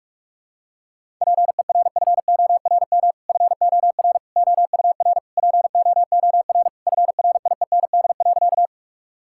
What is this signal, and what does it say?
Morse code, heard as a keyed tone, about 41 wpm, 700 Hz. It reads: PERFORM FOR OUR POOR FRIEND.
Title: Sentences from Top 1000 Words - Review Part 5 - 45wpm